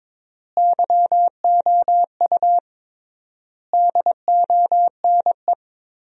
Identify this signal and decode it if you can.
Morse code, heard as a keyed tone, about 22 wpm, 700 Hz. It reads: YOU DONE
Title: 3x Word-Spacing - Sentences from Top 500 Words - 22wpm